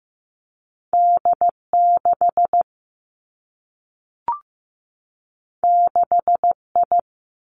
Morse code, heard as a keyed tone, about 15 wpm, 700 Hz. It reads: D 6 6 I
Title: Two Challenging Letter-Number Combinations 15wpm